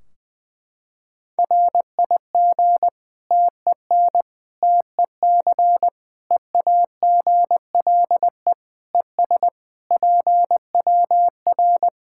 Morse code, heard as a keyed tone, about 20 wpm, 700 Hz. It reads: RIG TEN TEC EAGLE ES PWR
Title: K1USN SST QSO Practice 20wpm